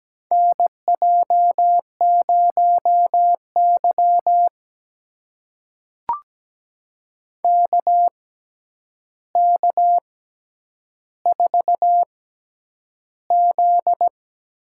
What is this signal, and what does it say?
Morse code, heard as a keyed tone, about 17 wpm, spaced as for 9 wpm, 700 Hz. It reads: NJ0Y KK4Z
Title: USA Call Signs - 6x Character Spacing - 17wpm